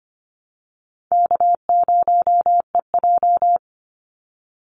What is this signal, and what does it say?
Morse code, heard as a keyed tone, about 25 wpm, 700 Hz. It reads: K0EJ
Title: Easy Call Signs Repeated 2x - 25wpm